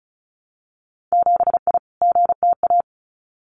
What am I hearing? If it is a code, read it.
Morse code, heard as a keyed tone, about 36 wpm, 700 Hz. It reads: 7I GTA
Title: Winter Field Day Exchanges Encore 35wpm